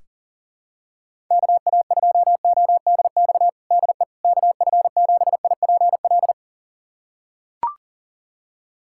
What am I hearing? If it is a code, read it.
Morse code, heard as a keyed tone, about 40 wpm, 700 Hz. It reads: KA2ODX DE KF7IPL
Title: USA Callsign DE Callsign - 6 Characters - 40wpm